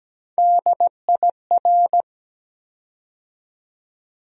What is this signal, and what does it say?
Morse code, heard as a keyed tone, about 17 wpm, 700 Hz. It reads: DIR
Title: USA Call Sign Postfixes - 3 Letters 17wpm